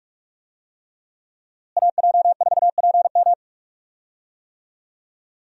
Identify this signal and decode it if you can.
Morse code, heard as a keyed tone, about 41 wpm, 700 Hz. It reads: AJ4PM
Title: USA Call Signs - 5 Characters - Repeated 3x - 45wpm